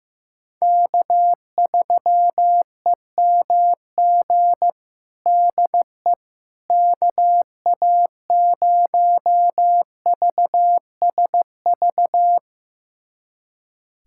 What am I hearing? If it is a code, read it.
Morse code, heard as a keyed tone, about 15 wpm, 700 Hz. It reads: K3EMG DE KA0VSV
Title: USA Callsign DE Callsign 15wpm